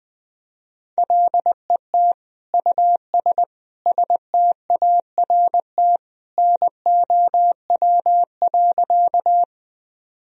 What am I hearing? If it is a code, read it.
Morse code, heard as a keyed tone, about 20 wpm, 700 Hz. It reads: LET US START NOW.
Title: Sentences from Top 1000 Words - Review Part 1 - 20wpm